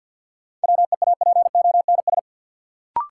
Morse code, heard as a keyed tone, about 52 wpm, 700 Hz. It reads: WEAPONS
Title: Sets of 5 Words Encore - New Word Drill - 50wpm